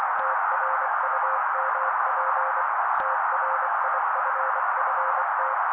Morse code, heard as a keyed tone, about 24 wpm, 530 Hz. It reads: TRUMP TRIFFT
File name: daily / 23wpm noise